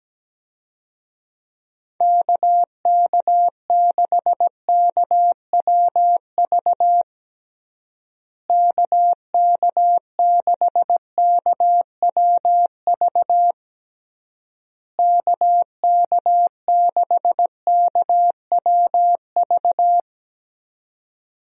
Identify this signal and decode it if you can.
Morse code, heard as a keyed tone, about 17 wpm, 700 Hz. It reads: KK6KWV KK6KWV KK6KWV
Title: USA Call Signs - 6 Characters - Repeated 3x - 17wpm